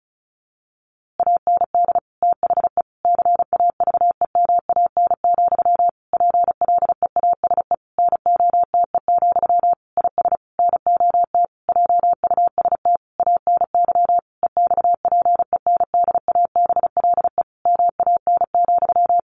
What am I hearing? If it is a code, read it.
Morse code, heard as a keyed tone, about 34 wpm, 700 Hz. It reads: AND THE CAVEMAN, PLEASE NOTE, IS NOT JUST ANY EXPENDABLE MAN,
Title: A Sound Of Thunder 35wpm